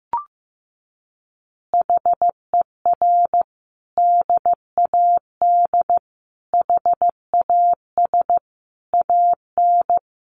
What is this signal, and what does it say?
Morse code, heard as a keyed tone, about 15 wpm, 700 Hz. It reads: HER DAD HAS AN
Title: Additional Easy Phrases 15wpm